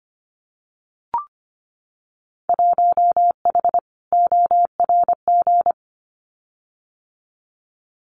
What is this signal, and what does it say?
Morse code, heard as a keyed tone, about 25 wpm, 700 Hz. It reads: 1H ORG
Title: Winter Field Day Exchanges 25wpm